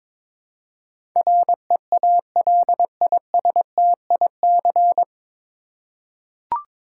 Morse code, heard as a keyed tone, about 21 wpm, 700 Hz. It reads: REALISTIC
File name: 9 Letter Words - First 3 Letters Given - 22wpm